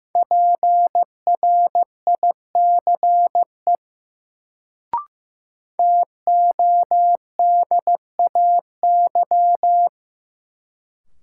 Morse code, heard as a keyed tone, about 15 wpm, 700 Hz. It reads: PRICE TODAY
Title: Sentences from Top 600 Words - New Word Drill - 15wpm